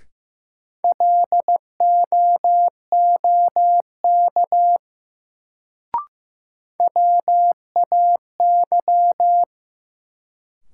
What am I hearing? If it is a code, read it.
Morse code, heard as a keyed tone, about 15 wpm, 700 Hz. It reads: LOOK WAY
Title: Sentences from Top 100 Words - New Word Drill - 15wpm